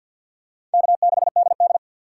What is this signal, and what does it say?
Morse code, heard as a keyed tone, about 46 wpm, 690 Hz. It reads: K6DD
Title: USA Call Signs - 6x Character Spacing - 50wpm